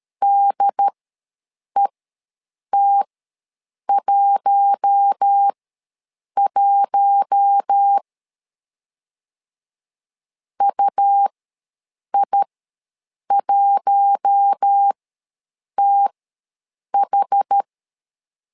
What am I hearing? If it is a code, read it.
Morse code, heard as a keyed tone, about 13 wpm, spaced as for 7 wpm, 800 Hz. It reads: DET11 UI1TH
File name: LEZIONI / 4 Q1 ws 12wpm 2129